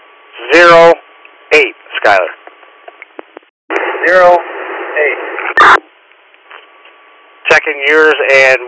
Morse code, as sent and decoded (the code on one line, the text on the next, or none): none